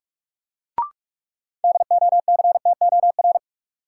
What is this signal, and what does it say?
Morse code, heard as a keyed tone, about 46 wpm, 700 Hz. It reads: DOCTOR
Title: Sentences from Top 1500 Words - New Word Drill Alt - 45wpm